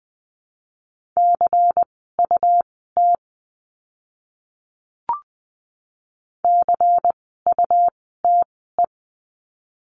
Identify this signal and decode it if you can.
Morse code, heard as a keyed tone, about 20 wpm, spaced as for 14 wpm, 700 Hz. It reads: CUT CUTE
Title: 101 Lesson 17f - Letter u - Words 20wpm